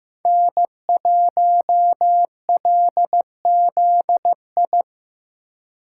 Morse code, heard as a keyed tone, about 15 wpm, 700 Hz. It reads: N1LZI
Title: USA Call Signs - 3x Character Spacing - 15wpm